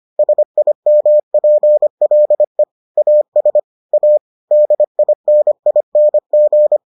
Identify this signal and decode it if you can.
Morse code, heard as a keyed tone, about 25 wpm, 600 Hz. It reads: SIMPLE AS A DINING